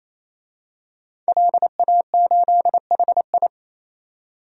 Morse code, heard as a keyed tone, about 28 wpm, 700 Hz. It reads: LA8HI
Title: DXCC-100 Call Signs - 5 Characters - 28wpm